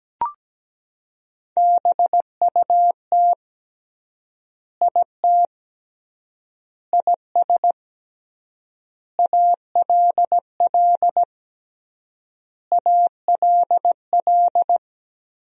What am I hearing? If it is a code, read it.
Morse code, heard as a keyed tone, about 17 wpm, 700 Hz. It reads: BUT IT IS ALL ALL
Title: 3x Word-Spacing - Sentences from Top 300 Words - 17wpm